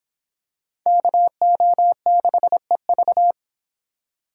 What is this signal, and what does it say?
Morse code, heard as a keyed tone, about 26 wpm, 700 Hz. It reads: KO6EV